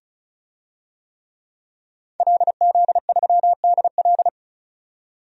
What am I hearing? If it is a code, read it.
Morse code, heard as a keyed tone, about 35 wpm, 700 Hz. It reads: LZ3DL
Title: DXCC-100 Call Signs - 5 Characters - 35wpm